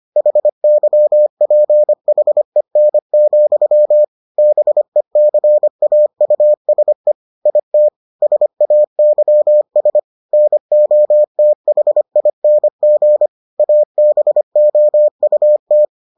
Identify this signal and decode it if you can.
Morse code, heard as a keyed tone, about 25 wpm, 600 Hz. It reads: HYPHEN, BECAUSE IT SAYS NOTHING ABOUT